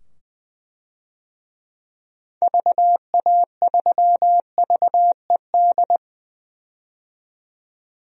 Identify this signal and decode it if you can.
Morse code, heard as a keyed tone, about 20 wpm, 700 Hz. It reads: VA3VED
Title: DXCC-100 Call Signs - 6 Characters - 20wpm